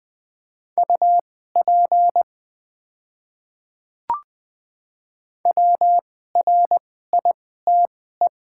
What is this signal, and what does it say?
Morse code, heard as a keyed tone, about 20 wpm, spaced as for 14 wpm, 700 Hz. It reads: UP WRITE
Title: 2x Character Spacing - Top 100 Words - 20wpm